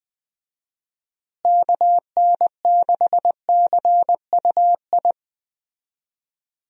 Morse code, heard as a keyed tone, about 20 wpm, 700 Hz. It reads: KN6CUI
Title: USA Call Signs - 6 Characters - Repeated 3x - 20wpm